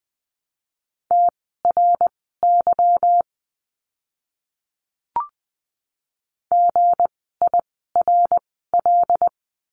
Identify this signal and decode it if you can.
Morse code, heard as a keyed tone, about 20 wpm, spaced as for 14 wpm, 700 Hz. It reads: TRY GIRL
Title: 160 Lesson 26f - Letter g - Words 20wpm